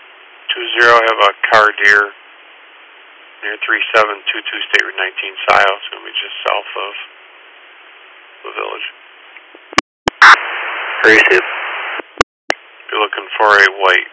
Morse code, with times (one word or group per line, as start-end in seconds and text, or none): none